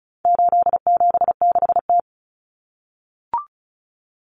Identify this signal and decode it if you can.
Morse code, heard as a keyed tone, about 34 wpm, 700 Hz. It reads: 876T
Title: Common Cut Numbers - 4-Digit 35wpm